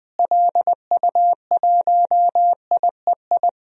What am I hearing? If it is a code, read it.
Morse code, heard as a keyed tone, about 21 wpm, 700 Hz. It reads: LU1IEI
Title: DXCC-100 Call Signs - 6 Characters - Repeated 2x - 20wpm